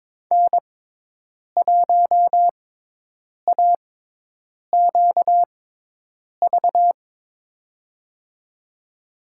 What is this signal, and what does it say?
Morse code, heard as a keyed tone, about 22 wpm, spaced as for 8 wpm, 700 Hz. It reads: N1AQV
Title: USA Call Signs - 5 Characters - 6x Character Spacing - 22wpm